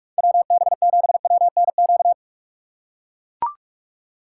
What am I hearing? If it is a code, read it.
Morse code, heard as a keyed tone, about 42 wpm, 700 Hz. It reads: WB7WNQ
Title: USA Call Signs - 6 Characters - 4x Character Spacing - 45wpm